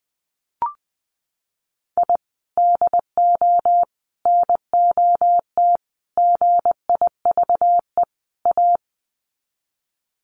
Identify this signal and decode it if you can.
Morse code, heard as a keyed tone, about 20 wpm, 700 Hz. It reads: I DO NOT GIVE A
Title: Sets of 5 Words Encore2 20wpm